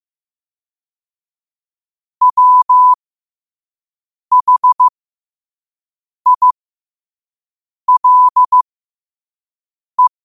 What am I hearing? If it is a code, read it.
Morse code, heard as a keyed tone, about 15 wpm, spaced as for 5 wpm, 1000 Hz. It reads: WHILE
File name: qotdfiles / 5wpm